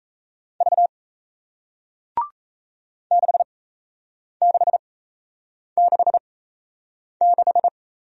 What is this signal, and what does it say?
Morse code, heard as a keyed tone, about 38 wpm, spaced as for 8 wpm, 700 Hz. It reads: V 6666